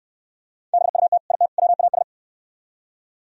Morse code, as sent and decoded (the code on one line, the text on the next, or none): -... ..- - / .. - / .-- .- ...
BUT IT WAS